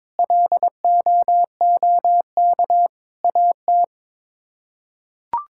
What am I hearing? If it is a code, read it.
Morse code, heard as a keyed tone, about 22 wpm, 700 Hz. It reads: LOOK AT
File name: Sets of 9 Words - 22wpm